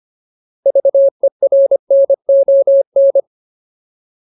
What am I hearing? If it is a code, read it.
Morse code, heard as a keyed tone, about 25 wpm, 550 Hz. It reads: VERNON
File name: Ultimate Word List / 25WPM Ultimate